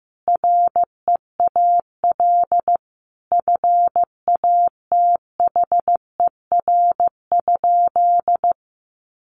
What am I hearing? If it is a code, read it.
Morse code, heard as a keyed tone, about 15 wpm, 700 Hz. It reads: REAL FATHER?
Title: Sentences from Top 300 Words 15wpm